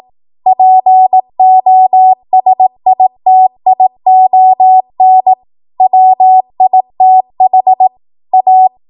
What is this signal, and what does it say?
Morse code, heard as a keyed tone, about 18 wpm, 750 Hz. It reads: POSITION WITH A
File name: Morse / 18WPM